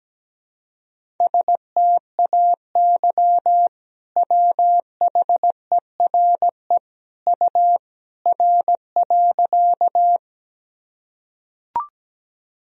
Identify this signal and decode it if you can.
Morse code, heard as a keyed tone, about 17 wpm, 700 Hz. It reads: STAY WHERE U R.
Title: Sentences from Top 500 Words and Abbreviations 17wpm